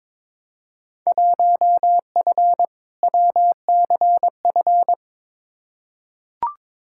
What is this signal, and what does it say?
Morse code, heard as a keyed tone, about 22 wpm, 700 Hz. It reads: 1F WCF
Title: Field Day Exchanges - 22wpm